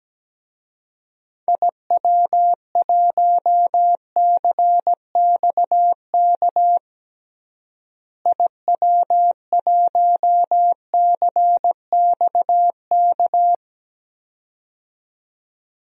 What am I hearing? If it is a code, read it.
Morse code, heard as a keyed tone, about 17 wpm, 700 Hz. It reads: IW1CXK IW1CXK
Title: DXCC-100 Call Signs - 6 Characters - Repeated 2x - 17wpm